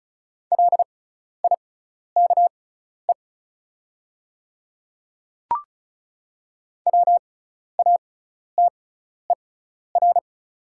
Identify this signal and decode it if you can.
Morse code, heard as a keyed tone, about 34 wpm, spaced as for 12 wpm, 690 Hz. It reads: LIKE WATER
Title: 6x Character Spacing - Top 500 Words - 35wpm